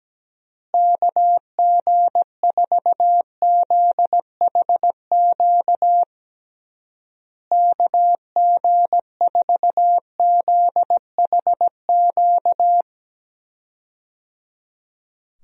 Morse code, heard as a keyed tone, about 17 wpm, 700 Hz. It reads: KG4ZHQ KG4ZHQ
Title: USA Call Signs - 6 Characters - Repeated 3x - 17wpm